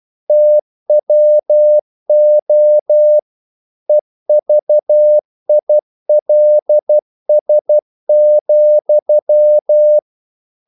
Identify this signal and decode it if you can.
Morse code, heard as a keyed tone, about 12 wpm, 600 Hz. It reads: TWO EVILS,
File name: CW / Practice / EnglishQuotations0312